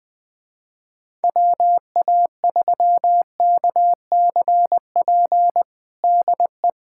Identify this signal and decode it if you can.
Morse code, heard as a keyed tone, about 20 wpm, 700 Hz. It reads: WA3KCP DE
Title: USA Callsign DE Callsign - 6 Characters - 20wpm